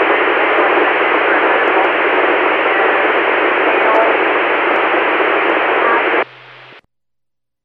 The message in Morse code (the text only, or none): none